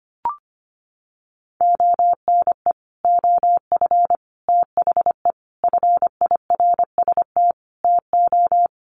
Morse code, heard as a keyed tone, about 25 wpm, 690 Hz. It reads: ONE OF THE FIRST TO
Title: Sets of 5 Words Encore 25wpm